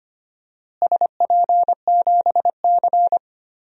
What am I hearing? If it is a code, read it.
Morse code, heard as a keyed tone, about 25 wpm, 700 Hz. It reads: SP7C
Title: DXCC-100 Call Signs - 4 Characters - Repeated 3x - 25wpm